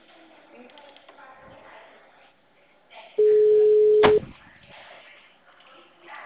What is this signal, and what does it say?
Ringback tone (Europe, 425 Hz).